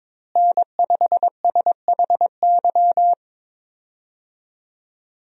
Morse code, heard as a keyed tone, about 22 wpm, 700 Hz. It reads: N5SHY
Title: USA Call Signs - 5 Characters - Repeated 2x - 22wpm